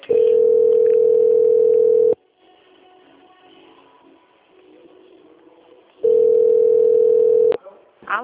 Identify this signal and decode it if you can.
Ringback tone (North America), 440 + 480 Hz, with speech.
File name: out- - - - - .17921